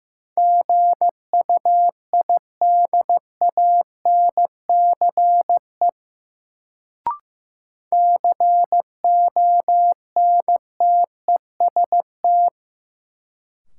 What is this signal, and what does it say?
Morse code, heard as a keyed tone, about 15 wpm, 700 Hz. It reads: GUIDANCE CONTEST
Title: Sentences from Top 1600 Words - New Word Drill - 15wpm